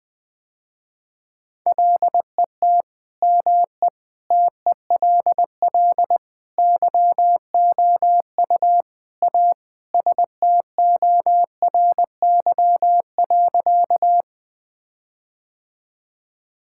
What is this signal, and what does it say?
Morse code, heard as a keyed tone, about 20 wpm, 700 Hz. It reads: LET ME TELL YOU A STORY.
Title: Sentences with 4-Word Phrases - Spoken Beginning - 20wpm